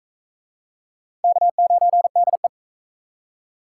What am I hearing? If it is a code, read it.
Morse code, heard as a keyed tone, about 38 wpm, 700 Hz. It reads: K9DE